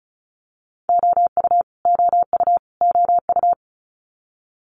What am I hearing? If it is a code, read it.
Morse code, heard as a keyed tone, about 36 wpm, 700 Hz. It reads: OU OU OU
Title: Multi-letter Phonemes Repeated 3x 35wpm